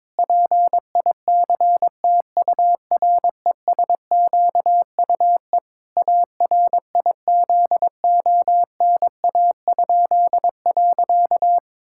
Morse code, heard as a keyed tone, about 22 wpm, 700 Hz. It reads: PICTURESQUE ARIZONA?.
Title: Pangram Challenge v2 - 22wpm